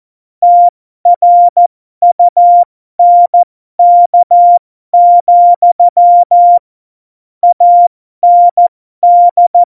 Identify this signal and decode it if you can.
Morse code, heard as a keyed tone, about 14 wpm, spaced as for 12 wpm, 700 Hz. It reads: TRUNK, AND